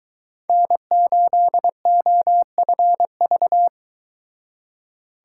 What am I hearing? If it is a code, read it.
Morse code, heard as a keyed tone, about 23 wpm, 700 Hz. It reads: N8OFV